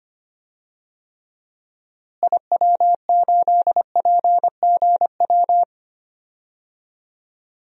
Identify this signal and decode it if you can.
Morse code, heard as a keyed tone, about 25 wpm, 700 Hz. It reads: IW8PGW
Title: DXCC-100 Call Signs - 6 Characters - Repeated 2x - 25wpm